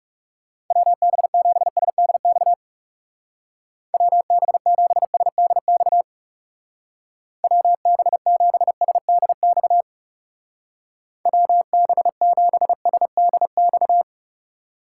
Morse code, heard as a keyed tone, about 40 wpm, 700 Hz. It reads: WB7SDX WB7SDX WB7SDX WB7SDX